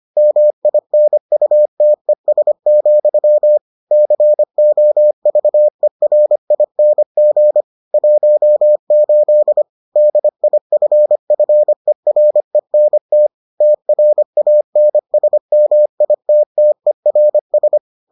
Morse code, heard as a keyed tone, about 25 wpm, 600 Hz. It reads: MINUTES, COVERING 18 DIFFERENT TRANSMITTERS